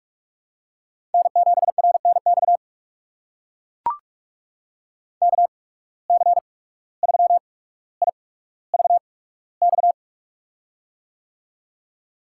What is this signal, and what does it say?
Morse code, heard as a keyed tone, about 43 wpm, spaced as for 18 wpm, 700 Hz. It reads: N7RNX KC3IVX
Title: USA Call Signs - 8x Character Spacing - 45wpm